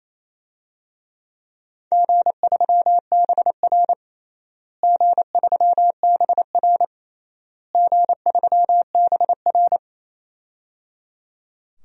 Morse code, heard as a keyed tone, about 28 wpm, 700 Hz. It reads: G3BR G3BR G3BR
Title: DXCC-100 Call Signs - 4 Characters - Repeated 3x - 28wpm